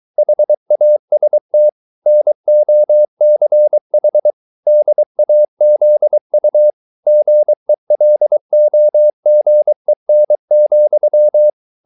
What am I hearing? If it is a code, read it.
Morse code, heard as a keyed tone, about 23 wpm, 600 Hz. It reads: HAST NOCH DAZU GELOGEN,